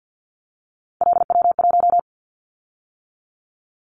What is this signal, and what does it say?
Morse code, heard as a keyed tone, about 50 wpm, 700 Hz. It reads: LW1
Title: Callsign Prefixes for Top 100 DXCC - 50wpm